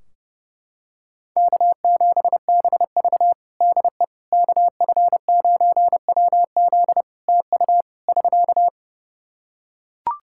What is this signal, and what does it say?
Morse code, heard as a keyed tone, about 32 wpm, 700 Hz. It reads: K7BV DE KF9WZ TU <SK>
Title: K1USN SST QSO Practice 30wpm